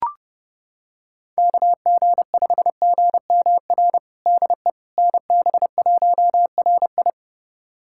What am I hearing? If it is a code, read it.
Morse code, heard as a keyed tone, about 32 wpm, 700 Hz. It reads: KG5GMR DE NB1RI
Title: USA Callsign DE Callsign 30wpm